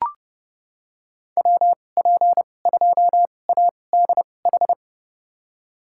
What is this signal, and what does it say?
Morse code, heard as a keyed tone, about 30 wpm, spaced as for 22 wpm, 700 Hz. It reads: WP2ADH
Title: USA Call Signs - 6 Characters - 2x Character Spacing - 30wpm